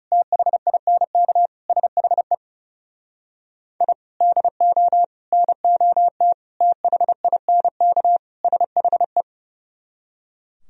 Morse code, heard as a keyed tone, about 37 wpm, 700 Hz. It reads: THINK SHE I DO NOT THINK SHE
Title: Speed-Racing - Sets of 5 Words 30wpm